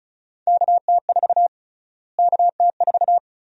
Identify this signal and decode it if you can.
Morse code, heard as a keyed tone, about 34 wpm, 700 Hz. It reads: KT4 KT4
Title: USA Call Sign Prefixes - Repeated 2x - 35wpm